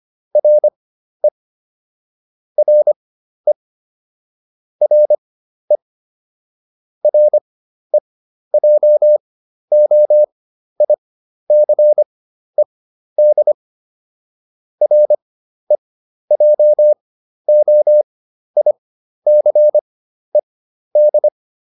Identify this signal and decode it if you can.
Morse code, heard as a keyed tone, about 25 wpm, spaced as for 12 wpm, 600 Hz. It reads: RE RE RE REJOICED REJOICED